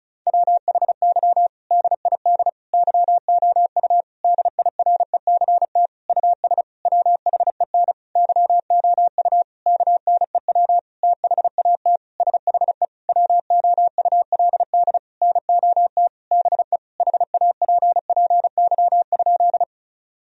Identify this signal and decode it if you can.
Morse code, heard as a keyed tone, about 34 wpm, 700 Hz. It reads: WHY DID YOU DIRECT US WHEN YOU KNEW THAT SHE WOULD NOT BE HAPPY?
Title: Sentences from Top 1000 Words - Review Part 2 - 35wpm